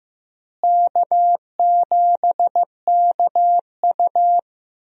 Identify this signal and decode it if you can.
Morse code, heard as a keyed tone, about 15 wpm, 700 Hz. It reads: K7KU
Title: USA Call Signs - 4 Characters - Repeated 2x - 15wpm